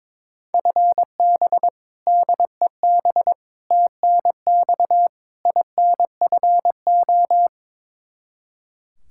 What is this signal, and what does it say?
Morse code, heard as a keyed tone, about 22 wpm, 700 Hz. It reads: FB DEB TNX INFO
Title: K1USN SST QSO Practice 22wpm